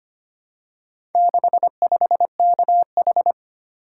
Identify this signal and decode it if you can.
Morse code, heard as a keyed tone, about 25 wpm, 700 Hz. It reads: 65KH
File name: practice / 25wpm / Four Letter-Number Combinations - 25wpm